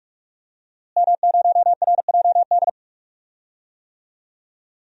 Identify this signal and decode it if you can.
Morse code, heard as a keyed tone, about 46 wpm, 700 Hz. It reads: M0RJD
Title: DXCC-100 Call Signs - 5 Characters - Repeated 3x - 45wpm